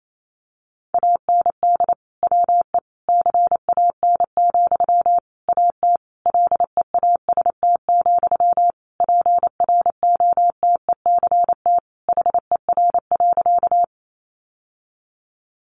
Morse code, heard as a keyed tone, about 28 wpm, 700 Hz. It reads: AND WE CAN, AT LEAST, PROTECT HER.
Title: Sentences from Top 900 Words 28wpm